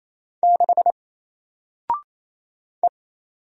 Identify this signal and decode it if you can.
Morse code, heard as a keyed tone, about 28 wpm, 700 Hz. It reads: 6 E
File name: Single-Letter-Number-28wpm